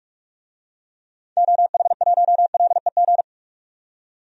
Morse code, heard as a keyed tone, about 43 wpm, 700 Hz. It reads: OH1LEG
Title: DXCC-100 Call Signs - 6 Characters - 45wpm